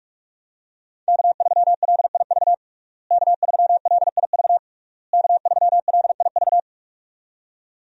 Morse code, heard as a keyed tone, about 46 wpm, 700 Hz. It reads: K3LIV K3LIV K3LIV
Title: USA Call Signs - 5 Characters - Repeated 3x - 45wpm